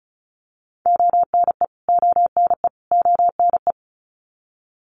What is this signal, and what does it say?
Morse code, heard as a keyed tone, about 34 wpm, 700 Hz. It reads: ONE ONE ONE
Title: Top 200 Words Repeated 35wpm